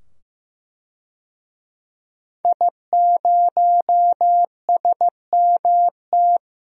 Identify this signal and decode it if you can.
Morse code, heard as a keyed tone, about 15 wpm, 700 Hz. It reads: I0SMT
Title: DXCC-100 Call Signs - 5 Characters - 15wpm